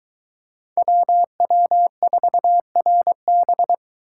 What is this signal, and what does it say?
Morse code, heard as a keyed tone, about 22 wpm, 700 Hz. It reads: WW4RB